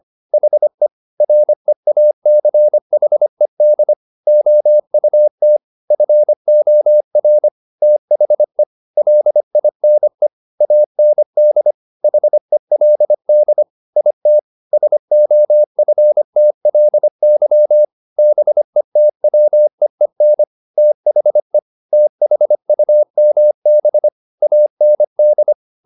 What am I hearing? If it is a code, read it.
Morse code, heard as a keyed tone, about 25 wpm, 600 Hz. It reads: HE REACHED OUT FOR THE LINE AND HELD IT SOFTLY BETWEEN THE THUMB AND
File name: the old man and the sea part 1 25wpm